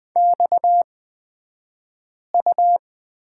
Morse code, heard as a keyed tone, about 20 wpm, spaced as for 5 wpm, 700 Hz. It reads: XU
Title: 228 Lesson 37d - Letter x - Two Characters 20wpm